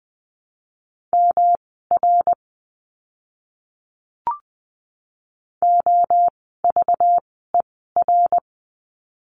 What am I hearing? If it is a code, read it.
Morse code, heard as a keyed tone, about 20 wpm, spaced as for 14 wpm, 700 Hz. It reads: MR OVER
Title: 244 Lesson 39f - Letter z - Words 20wpm